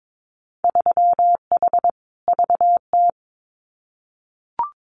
Morse code, heard as a keyed tone, about 22 wpm, 700 Hz. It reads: 3H VT
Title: Winter Field Day Exchanges Encore 22wpm